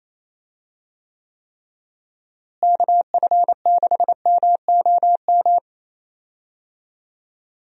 Morse code, heard as a keyed tone, about 28 wpm, 700 Hz. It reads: KF6MOM
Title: USA Call Signs - 6 Characters - Repeated 3x - 28wpm